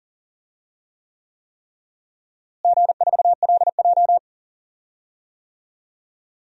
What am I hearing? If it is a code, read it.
Morse code, heard as a keyed tone, about 40 wpm, 700 Hz. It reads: G4LJ
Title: DXCC-100 Call Signs - 4 Characters - Repeated 3x - 40wpm